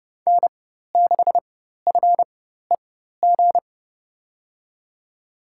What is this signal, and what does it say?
Morse code, heard as a keyed tone, about 30 wpm, spaced as for 14 wpm, 700 Hz. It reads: N6FEG
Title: USA Call Signs - 5 Characters - 4x Character Spacing - 30wpm